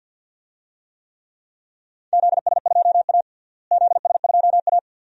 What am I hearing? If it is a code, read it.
Morse code, heard as a keyed tone, about 51 wpm, 700 Hz. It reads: ZS2A ZS2A
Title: DXCC-100 Call Signs - 4 Characters - Repeated 2x - 50wpm